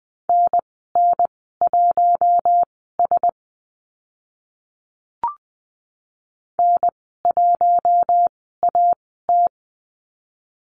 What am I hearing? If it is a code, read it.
Morse code, heard as a keyed tone, about 20 wpm, spaced as for 15 wpm, 700 Hz. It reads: NN1S N1AT
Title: 042 Lesson 8g - Number 1 - Call Signs 20wpm